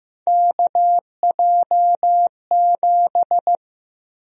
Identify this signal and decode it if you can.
Morse code, heard as a keyed tone, about 15 wpm, 700 Hz. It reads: KJ7
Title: USA Call Sign Prefixes - Repeated 2x - 15wpm